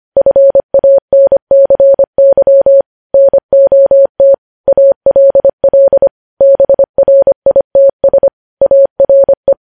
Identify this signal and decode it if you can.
Morse code, heard as a keyed tone, about 25 wpm, 550 Hz. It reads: FANCY NOT ALL BRITS ARE
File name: podcast / News headlines-1x-25x25